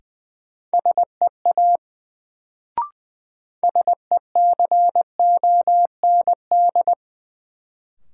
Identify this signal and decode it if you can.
Morse code, heard as a keyed tone, about 21 wpm, 700 Hz. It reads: SEA SECOND
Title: Sentences from Top 300 Words - New Word Drill - 20wpm